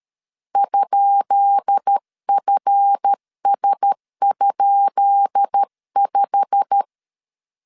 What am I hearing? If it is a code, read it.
Morse code, heard as a keyed tone, about 13 wpm, 800 Hz. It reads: ?FS?5
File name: LEZIONI / 10 SIGNS / 10B INT 12wpm 2414